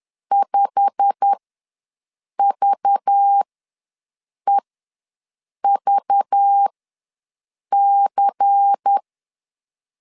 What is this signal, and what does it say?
Morse code, heard as a keyed tone, about 10 wpm, spaced as for 6 wpm, 800 Hz. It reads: 5VEVC
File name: LEZIONI / 8 YJ45 / 8 YJ45 ws 10wpm 3843